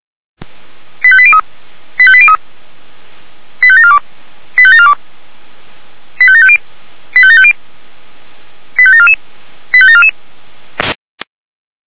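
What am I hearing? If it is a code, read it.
Five-tone selective call (ZVEI), heard as quick runs of five tones, tones of about 70 ms. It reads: 85803 85803 85742 85742 85750 85750 85744 85744
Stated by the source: Five-tone ZVEI selective call received by SDR, fire brigade Feuerwehr Orlamünde.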